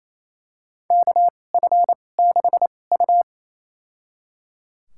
Morse code, heard as a keyed tone, about 28 wpm, spaced as for 20 wpm, 700 Hz. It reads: KF6U
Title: USA Call Signs - 4 Characters - 2x Character Spacing - 28wpm